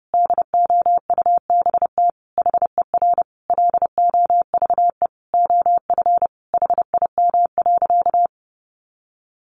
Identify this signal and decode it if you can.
Morse code, heard as a keyed tone, about 30 wpm, 700 Hz. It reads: DOUBT HER LOVE OF HIM.
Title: Sentences from Top 1500 Words 30wpm